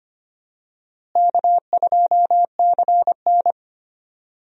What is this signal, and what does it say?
Morse code, heard as a keyed tone, about 25 wpm, 700 Hz. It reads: K2CN
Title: USA Call Signs - 4 Characters - Repeated 2x - 25wpm